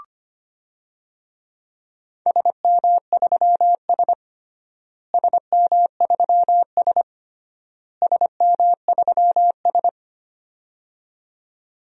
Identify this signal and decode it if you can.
Morse code, heard as a keyed tone, about 25 wpm, 700 Hz. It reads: SM3S SM3S SM3S
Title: DXCC-100 Call Signs - 4 Characters - Repeated 3x - 25wpm